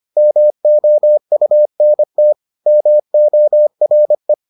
Morse code, heard as a keyed tone, about 25 wpm, 600 Hz. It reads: MOUNT MORE